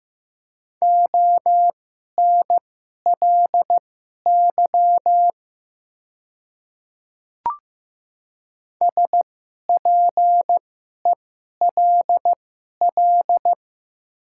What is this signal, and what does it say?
Morse code, heard as a keyed tone, about 15 wpm, spaced as for 11 wpm, 700 Hz. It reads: ONLY SPELL
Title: 2x Character Spacing - Top 300 Words - 15wpm